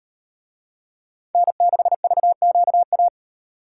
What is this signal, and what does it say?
Morse code, heard as a keyed tone, about 38 wpm, 700 Hz. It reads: N6VQA